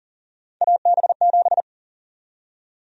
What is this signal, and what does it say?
Morse code, heard as a keyed tone, about 40 wpm, 700 Hz. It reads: AB7
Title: USA Call Sign Prefixes - 40wpm